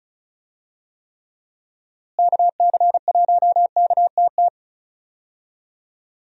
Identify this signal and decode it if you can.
Morse code, heard as a keyed tone, about 34 wpm, 700 Hz. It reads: KC1KTT
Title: USA Call Signs - 6 Characters - Repeated 3x - 35wpm